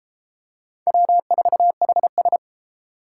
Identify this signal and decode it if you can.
Morse code, heard as a keyed tone, about 32 wpm, 700 Hz. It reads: W4HS